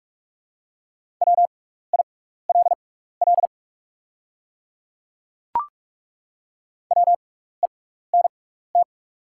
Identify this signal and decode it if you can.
Morse code, heard as a keyed tone, about 42 wpm, spaced as for 15 wpm, 690 Hz. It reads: WILL WENT
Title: 6x Character Spacing - Top 500 Words - 45wpm